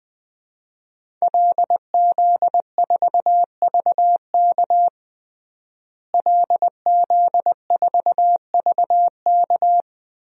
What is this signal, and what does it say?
Morse code, heard as a keyed tone, about 20 wpm, 700 Hz. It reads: LZ4VK LZ4VK
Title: DXCC-100 Call Signs - 5 Characters - Repeated 3x - 20wpm